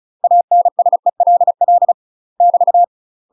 Morse code, heard as a keyed tone, about 34 wpm, 700 Hz. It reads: ANSELL <BT>
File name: UK MASTER QSOs TRAINING / 35WPM / UK MASTER QSOs 0034